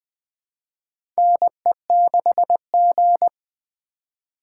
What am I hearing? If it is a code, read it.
Morse code, heard as a keyed tone, about 20 wpm, 700 Hz. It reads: NE6G
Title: USA Call Signs - 4 Characters - 4x Character Spacing - 20wpm